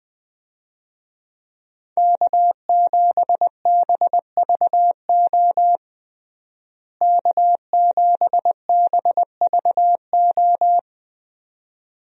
Morse code, heard as a keyed tone, about 20 wpm, 700 Hz. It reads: K7BVO K7BVO
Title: USA Call Signs - 5 Characters - Repeated 2x - 20wpm